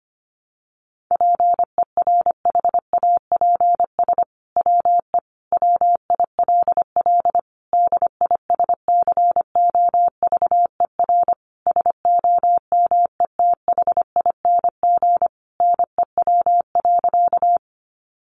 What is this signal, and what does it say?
Morse code, heard as a keyed tone, about 25 wpm, 700 Hz. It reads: PERHAPS WE WILL DISCOVER SOMETHING NEW.
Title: Sentences from Top 1100 Words 25wpm